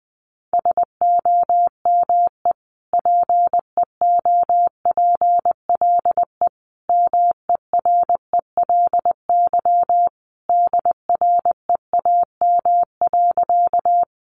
Morse code, heard as a keyed tone, about 20 wpm, 700 Hz. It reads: SOME PEOPLE MERELY DREAM.
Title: Sentences from Top 1500 Words 20wpm